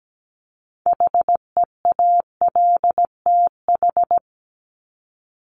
Morse code, heard as a keyed tone, about 17 wpm, 700 Hz. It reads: HEALTH
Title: Top 501-750 Words 17wpm